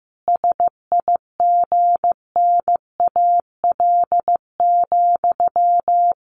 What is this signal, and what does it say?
Morse code, heard as a keyed tone, about 15 wpm, 700 Hz. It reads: SIGNAL,
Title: Sentences from Top 1400 Words 15wpm